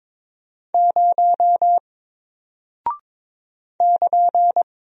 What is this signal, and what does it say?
Morse code, heard as a keyed tone, about 22 wpm, 700 Hz. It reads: 0 <KN>
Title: Single Character (Letter:Number:Punctuation:Prosign) Alt 22wpm